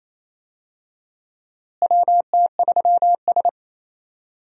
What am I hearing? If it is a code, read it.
Morse code, heard as a keyed tone, about 27 wpm, 700 Hz. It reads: WT3S
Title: USA Call Signs - 4 Characters - Repeated 3x - 28wpm